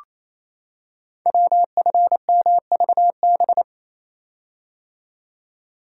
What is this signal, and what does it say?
Morse code, heard as a keyed tone, about 28 wpm, 700 Hz. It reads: WFMVB
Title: Code Groups - 5 Letters - 28wpm